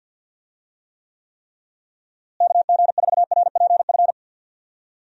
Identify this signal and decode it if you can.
Morse code, heard as a keyed tone, about 48 wpm, 700 Hz. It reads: KG4RPF
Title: USA Call Signs - 6 Characters - Repeated 3x - 50wpm